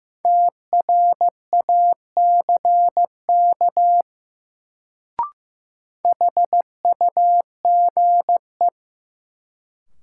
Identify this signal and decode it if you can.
Morse code, heard as a keyed tone, about 15 wpm, 700 Hz. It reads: TRACK HUGE
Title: Sentences from Top 1100 Words - New Word Drill Alt - 15wpm